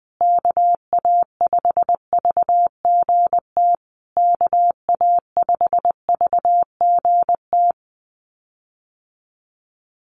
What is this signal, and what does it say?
Morse code, heard as a keyed tone, about 20 wpm, 700 Hz. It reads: KA5VGT KA5VGT
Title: USA Call Signs Repeated 3x 20wpm